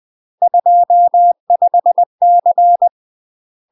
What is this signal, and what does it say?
Morse code, heard as a keyed tone, about 20 wpm, 700 Hz. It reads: 25C
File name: UK MASTER QSOs TRAINING / 20WPM / UK MASTER QSOs 0025